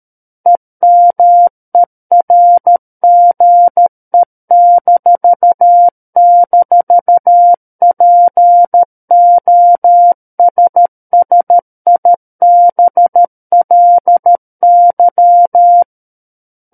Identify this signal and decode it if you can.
Morse code, heard as a keyed tone, about 13 wpm, 700 Hz. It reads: EMERGE--POSSIBLY
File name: Book One / 04-Chapter 4 - 13 WPM